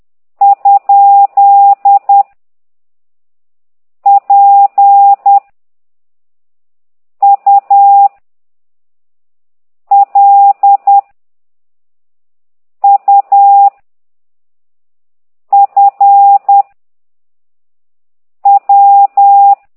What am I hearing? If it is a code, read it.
Morse code, heard as a keyed tone, about 10 wpm, 800 Hz. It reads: ? P U L U F W
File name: ct2idw / exerc8